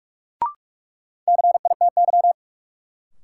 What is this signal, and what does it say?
Morse code, heard as a keyed tone, about 47 wpm, 700 Hz. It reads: CITY
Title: Sentences from Top 300 Words - New Word Drill - 45wpm